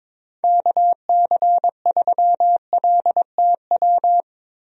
Morse code, heard as a keyed tone, about 22 wpm, 700 Hz. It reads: KC3LTW
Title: USA Call Signs - 6 Characters - Repeated 3x - 22wpm